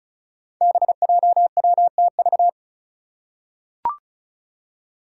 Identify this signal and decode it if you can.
Morse code, heard as a keyed tone, about 34 wpm, 700 Hz. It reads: BJWTV
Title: Five Challenging Letter-Number Combinations 35wpm